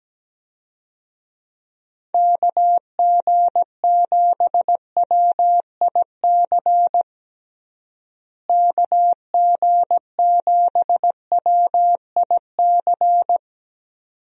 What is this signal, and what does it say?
Morse code, heard as a keyed tone, about 17 wpm, 700 Hz. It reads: KG7WIC KG7WIC
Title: USA Call Signs - 6 Characters - Repeated 2x - 17wpm